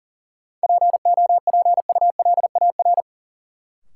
Morse code, heard as a keyed tone, about 38 wpm, 700 Hz. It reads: POPULAR
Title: Sentences from Top 1000 Words - Review Part 3 - New Word Drill - 40wpm